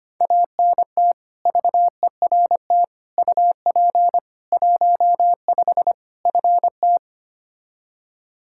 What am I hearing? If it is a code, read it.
Morse code, heard as a keyed tone, about 25 wpm, 700 Hz. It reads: ANT VERT UP 15 FT
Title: K1USN SST QSO Practice 25wpm